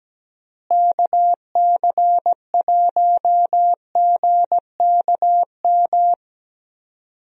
Morse code, heard as a keyed tone, about 17 wpm, 700 Hz. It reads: KC1GKM
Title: USA Call Signs - 6 Characters - Repeated 2x - 17wpm